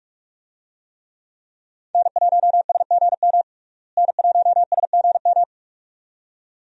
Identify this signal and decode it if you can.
Morse code, heard as a keyed tone, about 46 wpm, 700 Hz. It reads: N1SGM N1SGM
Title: USA Call Signs - 5 Characters - Repeated 2x - 45wpm